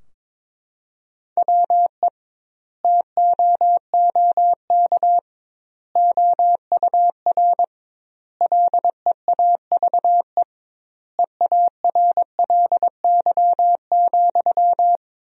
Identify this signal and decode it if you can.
Morse code, heard as a keyed tone, about 22 wpm, 700 Hz. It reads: WE TOOK OUR LEAVE EARLY,
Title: 2x Word-Spacing - Sentences from Top 400 Words - 22wpm